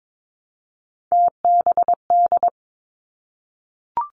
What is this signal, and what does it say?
Morse code, heard as a keyed tone, about 22 wpm, 700 Hz. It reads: TBD
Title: Three Challenging Letter-Number Combinations 22wpm